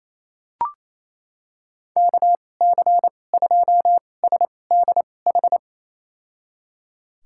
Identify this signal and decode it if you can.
Morse code, heard as a keyed tone, about 28 wpm, spaced as for 20 wpm, 700 Hz. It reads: KC2SDH
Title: USA Call Signs - 2x Character Spacing - 28wpm